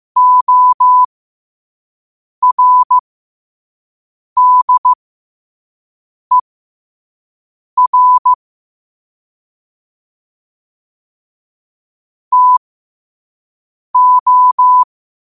Morse code, heard as a keyed tone, about 15 wpm, spaced as for 5 wpm, 1000 Hz. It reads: ORDER TO
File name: 5wpm